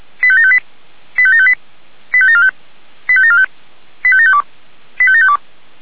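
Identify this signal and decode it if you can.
Five-tone selective call (ZVEI), heard as quick runs of five tones, tones of about 70 ms. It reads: 85758 85758 85745 85745 85752 85752
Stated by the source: ZVEI five-tone selective call 85726, received by SDR.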